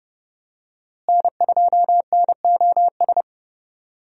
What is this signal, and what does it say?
Morse code, heard as a keyed tone, about 30 wpm, 700 Hz. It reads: N2NOS
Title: USA Call Signs - 5 Characters - Repeated 3x - 30wpm